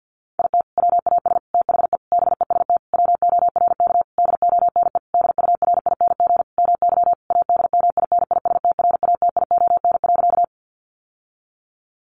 Morse code, heard as a keyed tone, about 50 wpm, 700 Hz. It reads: IT WAS THE BEST WORK DONE DURING MY ADMINISTRATION.
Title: Sentences from Top 1300 Words 50wpm